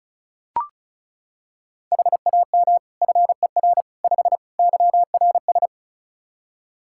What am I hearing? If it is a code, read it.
Morse code, heard as a keyed tone, about 36 wpm, 700 Hz. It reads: HAM FER 5 YRS
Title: Practice QSO - Exchange 3 - More Info - 35wpm